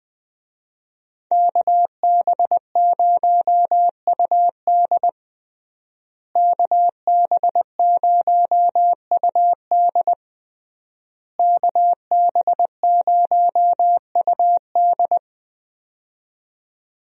Morse code, heard as a keyed tone, about 20 wpm, 700 Hz. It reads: KB0UD KB0UD KB0UD
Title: USA Call Signs - 5 Characters - Repeated 3x - 20wpm